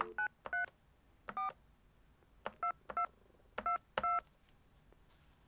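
DTMF, heard as telephone touch tones, tones of about 120 ms. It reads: 934 2222
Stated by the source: European touch-tone telephone dialing from the Digiffects Sound Effects Library.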